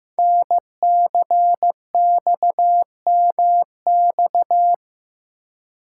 Morse code, heard as a keyed tone, about 15 wpm, 700 Hz. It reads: NCXMX
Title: Code Groups - 5 Letters - 15wpm